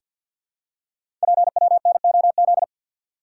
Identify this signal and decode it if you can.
Morse code, heard as a keyed tone, about 50 wpm, 690 Hz. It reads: PWNOZ
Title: Code Groups - 5 Letters - 50wpm